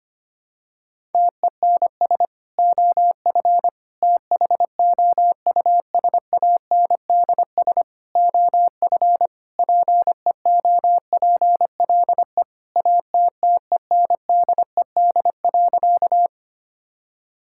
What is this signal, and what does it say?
Morse code, heard as a keyed tone, about 25 wpm, 700 Hz. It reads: TENS OF THOUSANDS OF PEOPLE ATTENDED.
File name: practice / 25wpm / Sentences with 4-Word Phrases - 25wpm